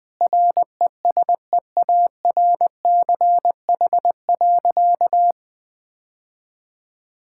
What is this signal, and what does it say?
Morse code, heard as a keyed tone, about 21 wpm, 700 Hz. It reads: RESEARCH.
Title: Sentences from Top 1000 Words - Review Part 5 - 20wpm